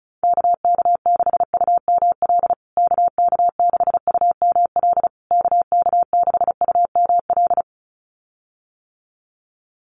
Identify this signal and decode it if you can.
Morse code, heard as a keyed tone, about 34 wpm, 700 Hz. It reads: KK6UML KK6UML KK6UML
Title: USA Call Signs Repeated 3x 35wpm